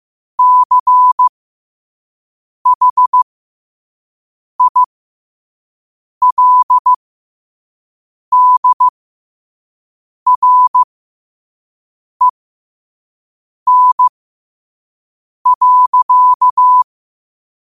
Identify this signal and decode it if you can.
Morse code, heard as a keyed tone, about 15 wpm, spaced as for 5 wpm, 1000 Hz. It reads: CHILDREN.
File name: qotdfiles / 5wpm